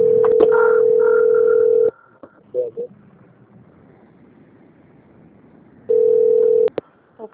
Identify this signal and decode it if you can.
Ringback tone (North America), 440 + 480 Hz, with speech.
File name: out- - - - - .31416